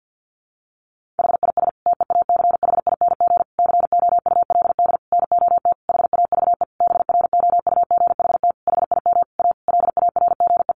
Morse code, heard as a keyed tone, about 50 wpm, 700 Hz. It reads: HIS TEACHING COULD NOT HAVE BROUGHT HIM A LARGE
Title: Sentences from Top 1100 Words 50wpm